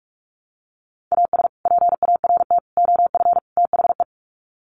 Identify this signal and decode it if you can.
Morse code, heard as a keyed tone, about 44 wpm, 700 Hz. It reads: AS PART OF THE
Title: Sets of 4 Words 45wpm